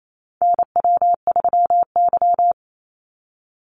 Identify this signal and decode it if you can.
Morse code, heard as a keyed tone, about 28 wpm, 700 Hz. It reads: NW3Y
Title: Call Signs - 28wpm